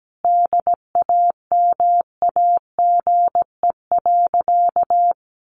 Morse code, heard as a keyed tone, about 17 wpm, 700 Hz. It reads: DAMAGE.
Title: Sentences from Top 1100 Words 17wpm